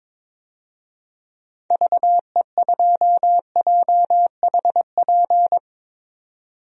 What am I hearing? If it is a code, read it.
Morse code, heard as a keyed tone, about 22 wpm, 700 Hz. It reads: VE2JHP
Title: DXCC-100 Call Signs - 6 Characters - 22wpm